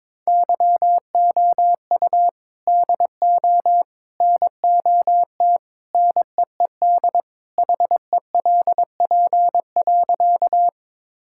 Morse code, heard as a keyed tone, about 22 wpm, 700 Hz. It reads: YOU DO NOT NEED HELP.
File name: practice / 22wpm / Sentences with 3-Word Phrases - 22wpm